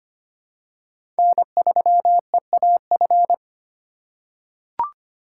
Morse code, heard as a keyed tone, about 25 wpm, 700 Hz. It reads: N3EAF
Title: USA Call Signs - 5 Characters - 2x Character Spacing - 25wpm